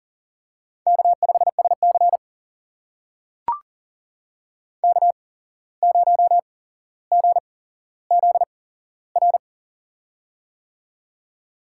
Morse code, heard as a keyed tone, about 40 wpm, spaced as for 16 wpm, 700 Hz. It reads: K5SC K0GZR
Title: USA Call Signs - 8x Character Spacing - 40wpm